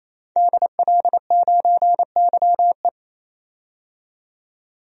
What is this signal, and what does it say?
Morse code, heard as a keyed tone, about 28 wpm, 700 Hz. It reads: DL9YE
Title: DXCC-100 Call Signs - 5 Characters - 28wpm